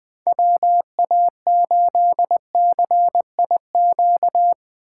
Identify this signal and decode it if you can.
Morse code, heard as a keyed tone, about 20 wpm, 700 Hz. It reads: WA8CIQ
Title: Speed Racing - USA Call Signs 17wpm